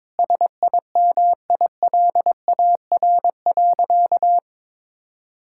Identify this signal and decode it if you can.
Morse code, heard as a keyed tone, about 21 wpm, 700 Hz. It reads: SIMILAR.
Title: Sentences from Top 1000 Words - Review Part 5 - 22wpm